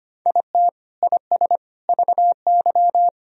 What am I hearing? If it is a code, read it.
Morse code, heard as a keyed tone, about 25 wpm, 700 Hz. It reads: IT IS VY
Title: Sentences from Top 500 Words and Abbreviations 25wpm